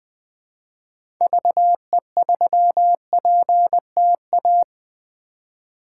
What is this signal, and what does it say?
Morse code, heard as a keyed tone, about 20 wpm, 700 Hz. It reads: VE3PTA
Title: 2023 POTA Hunter Call Signs - 20wpm